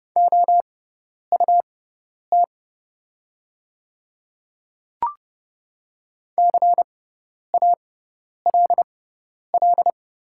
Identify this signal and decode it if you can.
Morse code, heard as a keyed tone, about 30 wpm, spaced as for 10 wpm, 690 Hz. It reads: OUT CALL
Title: 6x Character Spacing - Top 200 Words - 30wpm